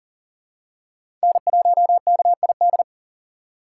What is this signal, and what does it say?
Morse code, heard as a keyed tone, about 37 wpm, 700 Hz. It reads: N1KID
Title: USA Call Signs - 5 Characters - 2x Character Spacing - 40wpm